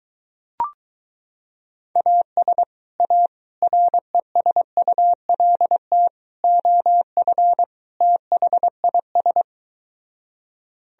Morse code, heard as a keyed tone, about 23 wpm, 700 Hz. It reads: AS A RESULT OF THIS